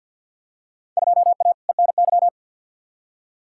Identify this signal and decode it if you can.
Morse code, heard as a keyed tone, about 47 wpm, 700 Hz. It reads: 2A ENY
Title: Field Day Exchanges - 50wpm